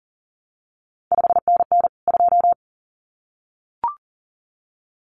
Morse code, heard as a keyed tone, about 42 wpm, 700 Hz. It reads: 5NN 2
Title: CQ WW DX Contest Exchange Encore - 40wpm